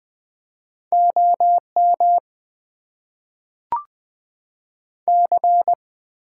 Morse code, heard as a keyed tone, about 20 wpm, 700 Hz. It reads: OM C
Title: Top 90 QSO Elements - 6x Character Spacing - 20wpm